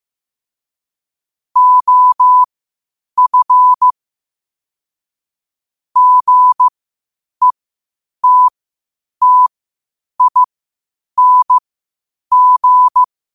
Morse code, heard as a keyed tone, about 15 wpm, spaced as for 8 wpm, 1000 Hz. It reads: OF GETTING
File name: qotdfiles / 8wpm